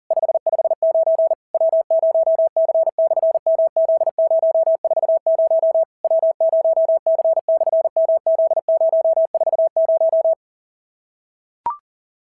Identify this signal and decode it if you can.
Morse code, heard as a keyed tone, about 40 wpm, 690 Hz. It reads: 559 W0C/MZ040 W0C/MZ040
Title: SOTA QSO Practice 40wpm